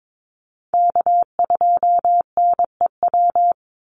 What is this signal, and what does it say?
Morse code, heard as a keyed tone, about 22 wpm, 700 Hz. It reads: K2NEW
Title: USA Call Signs 22wpm